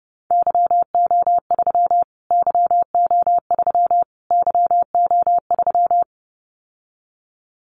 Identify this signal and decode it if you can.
Morse code, heard as a keyed tone, about 30 wpm, 700 Hz. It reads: YO3 YO3 YO3
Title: Callsign Prefixes for Top 25 DXCC - Repeated 3x - 30wpm